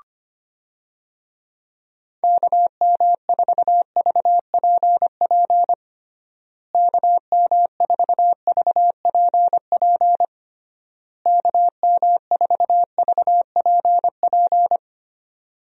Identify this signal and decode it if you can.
Morse code, heard as a keyed tone, about 25 wpm, 700 Hz. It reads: KM4VPP KM4VPP KM4VPP
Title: USA Call Signs - 6 Characters - Repeated 3x - 25wpm